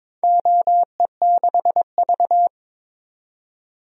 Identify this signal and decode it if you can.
Morse code, heard as a keyed tone, about 22 wpm, 700 Hz. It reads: OE6V
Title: DXCC-100 Call Signs - 4 Characters - 22wpm